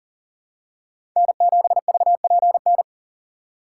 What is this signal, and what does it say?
Morse code, heard as a keyed tone, about 37 wpm, 700 Hz. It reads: N7VPN
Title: USA Call Signs - 5 Characters - 4x Character Spacing - 40wpm